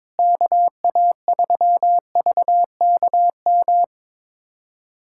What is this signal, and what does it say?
Morse code, heard as a keyed tone, about 22 wpm, 700 Hz. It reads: KA3VKM
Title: USA Callsign DE Callsign - 6 Characters - 22wpm